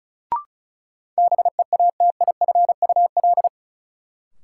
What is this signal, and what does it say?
Morse code, heard as a keyed tone, about 34 wpm, 700 Hz. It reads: BEATIFUL
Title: Sentences from Top 1300 Words - New Word Drill - 35wpm